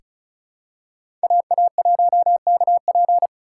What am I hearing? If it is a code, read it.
Morse code, heard as a keyed tone, about 36 wpm, 700 Hz. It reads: AA1KP
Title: USA Call Signs - 5 Characters - 3x Character Spacing - 35wpm